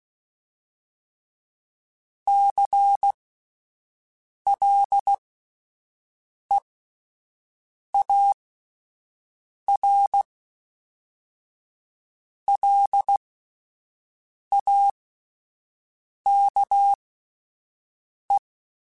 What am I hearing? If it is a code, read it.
Morse code, heard as a keyed tone, about 16 wpm, spaced as for 5 wpm, 800 Hz. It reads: CLEARLAKE
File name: fromSMs / FromSMs mp3s / Cities@5